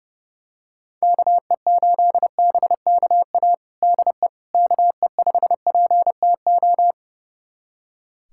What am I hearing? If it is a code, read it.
Morse code, heard as a keyed tone, about 32 wpm, 700 Hz. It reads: KE8BKA DE KE5PTO
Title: USA Callsign DE Callsign - 6 Characters - 30wpm